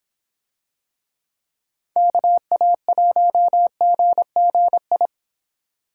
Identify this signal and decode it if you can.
Morse code, heard as a keyed tone, about 25 wpm, 700 Hz. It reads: KA1GGI